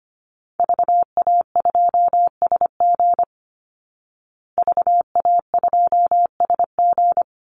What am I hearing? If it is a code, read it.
Morse code, heard as a keyed tone, about 25 wpm, 700 Hz. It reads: VA2SG VA2SG
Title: Call Signs Repeated 2x - 25wpm